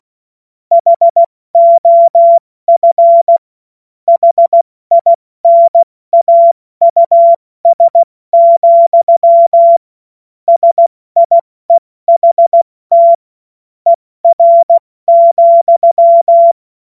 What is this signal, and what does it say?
Morse code, heard as a keyed tone, about 16 wpm, spaced as for 14 wpm, 680 Hz. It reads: HOF HINAUS, SIEHT ER,
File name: Witz-08-Juli-2025-70-F80